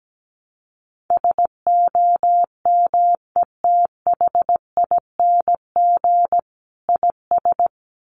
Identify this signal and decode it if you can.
Morse code, heard as a keyed tone, about 17 wpm, 700 Hz. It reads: SOMETHING IS
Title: Sentences from Top 600 Words 17wpm